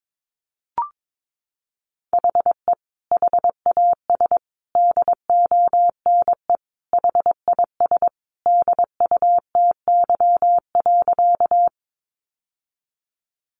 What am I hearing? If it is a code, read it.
Morse code, heard as a keyed tone, about 22 wpm, 700 Hz. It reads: HE HAS DONE HIS DUTY.
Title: Sentences from Top 1500 Words 22wpm